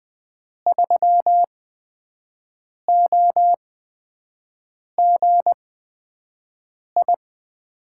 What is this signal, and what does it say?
Morse code, heard as a keyed tone, about 20 wpm, spaced as for 5 wpm, 700 Hz. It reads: 3OGI
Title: USA Call Signs - 6 Characters - 8x Character Spacing - 20wpm